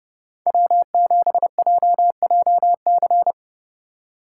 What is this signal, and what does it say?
Morse code, heard as a keyed tone, about 30 wpm, 700 Hz. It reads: W7JJC